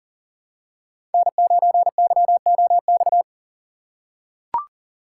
Morse code, heard as a keyed tone, about 40 wpm, 700 Hz. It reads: N9YOX
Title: USA Call Signs - 5 Characters - 4x Character Spacing - 40wpm